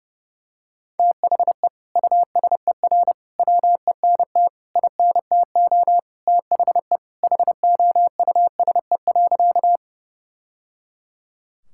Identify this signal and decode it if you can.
Morse code, heard as a keyed tone, about 31 wpm, 700 Hz. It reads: THE USER WENT INTO THE HOUSE.
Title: Sentences from Top 1000 Words - Review Part 5 - 30wpm